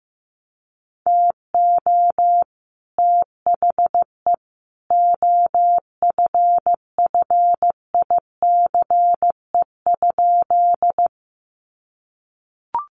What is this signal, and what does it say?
Morse code, heard as a keyed tone, about 15 wpm, 700 Hz. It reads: TO THE OFFICE?
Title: Sentences from Top 1000 Words 15wpm